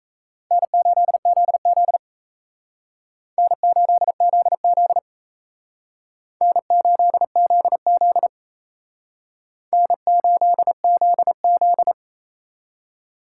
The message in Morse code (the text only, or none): N8ZZ N8ZZ N8ZZ N8ZZ